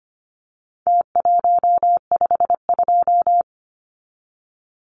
Morse code, heard as a keyed tone, about 25 wpm, 700 Hz. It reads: T152
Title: Numbers - Contest Style - 4-Digit 25wpm